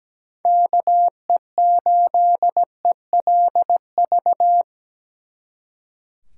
Morse code, heard as a keyed tone, about 21 wpm, 700 Hz. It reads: KE8ELV
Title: Speed Racing - USA 2x3 Call Signs 17wpm